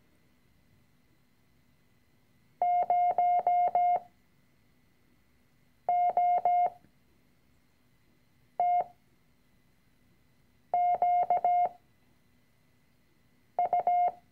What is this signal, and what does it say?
Morse code, heard as a keyed tone, about 17 wpm, spaced as for 4 wpm, 710 Hz. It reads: 0OTQU
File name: CW-Online-Kurs / Anfänger / Lektion 006